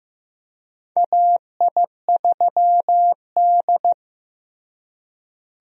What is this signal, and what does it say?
Morse code, heard as a keyed tone, about 15 wpm, 700 Hz. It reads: AI3D
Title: USA Call Signs - 4 Characters - Repeated 3x - 15wpm